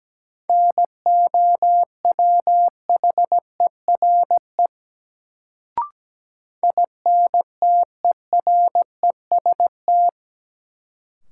Morse code, heard as a keyed tone, about 17 wpm, 700 Hz. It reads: NOWHERE INTEREST
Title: Sets of 5 Words Encore - New Word Drill - 17wpm